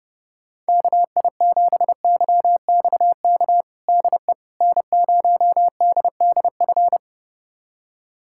Morse code, heard as a keyed tone, about 32 wpm, 700 Hz. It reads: KI7YXK DE N0DDF
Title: USA Callsign DE Callsign 30wpm